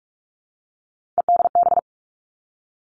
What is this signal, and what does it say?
Morse code, heard as a keyed tone, about 46 wpm, 700 Hz. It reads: EDB
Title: Three Challenging Letter-Number Combinations 45wpm